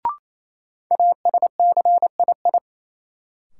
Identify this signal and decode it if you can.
Morse code, heard as a keyed tone, about 28 wpm, 700 Hz. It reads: ASCII
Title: Sets of 5 Words - New Word Drill - 28wpm